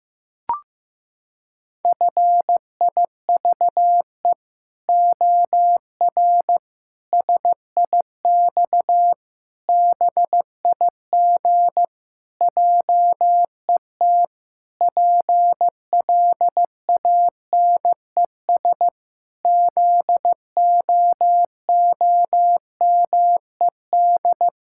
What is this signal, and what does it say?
Morse code, heard as a keyed tone, about 15 wpm, 700 Hz. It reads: FIVE OR SIX BIG JET PLANES ZOOMED
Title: Pangram Challenge v2 - 15wpm